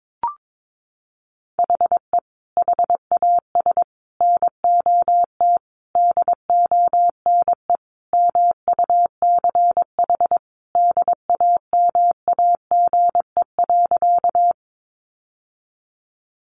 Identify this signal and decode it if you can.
Morse code, heard as a keyed tone, about 22 wpm, 700 Hz. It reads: HE HAS NOT DONE MUCH DAMAGE.
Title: Sentences from Top 1100 Words 22wpm